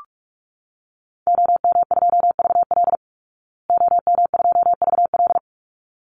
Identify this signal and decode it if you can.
Morse code, heard as a keyed tone, about 44 wpm, 700 Hz. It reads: OM2VL OM2VL
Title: Call Signs Repeated 3x - 45wpm